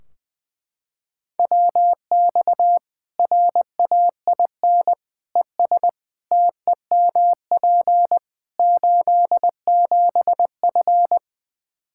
Morse code, heard as a keyed tone, about 20 wpm, 700 Hz. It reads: WX RAIN ES TEMP 87F
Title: Practice QSO - Exchange 2 - Info - 20wpm